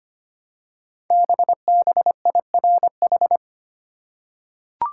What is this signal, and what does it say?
Morse code, heard as a keyed tone, about 25 wpm, 700 Hz. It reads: BBIRH
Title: Code Groups - 5 Letters - 25wpm